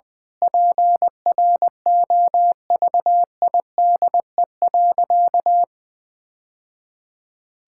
Morse code, heard as a keyed tone, about 21 wpm, 700 Hz. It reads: PROVIDE.
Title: Sentences from Top 1000 Words - Review Part 6 - 20wpm